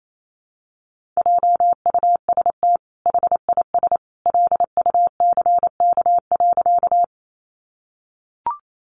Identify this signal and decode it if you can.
Morse code, heard as a keyed tone, about 28 wpm, 700 Hz. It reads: JUST HIS LUCK.
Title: Sentences from Top 1500 Words 28wpm